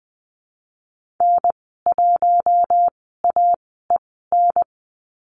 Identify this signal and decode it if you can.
Morse code, heard as a keyed tone, about 20 wpm, spaced as for 14 wpm, 700 Hz. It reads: N1AEN
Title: 055 Lesson 10g - Letter r - Call Signs 20wpm